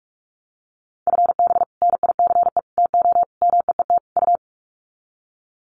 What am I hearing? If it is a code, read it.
Morse code, heard as a keyed tone, about 44 wpm, 700 Hz. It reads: FB NICE TO MEET U
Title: Common QSO Phrases 45wpm